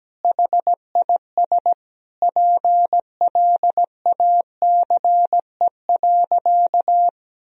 Morse code, heard as a keyed tone, about 17 wpm, 700 Hz. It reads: HIS PLACE.
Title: Sentences from Top 1000 Words - Review Part 1 - 17wpm